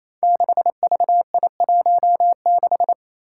Morse code, heard as a keyed tone, about 28 wpm, 700 Hz. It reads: 6VI16
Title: Five Challenging Letter-Number Combinations 28wpm